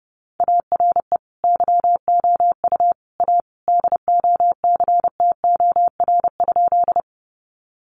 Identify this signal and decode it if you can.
Morse code, heard as a keyed tone, about 30 wpm, 700 Hz. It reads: ARE YOU A DOCTOR?
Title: Sentences from Top 1500 Words 30wpm